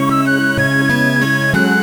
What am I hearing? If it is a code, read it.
There is no code.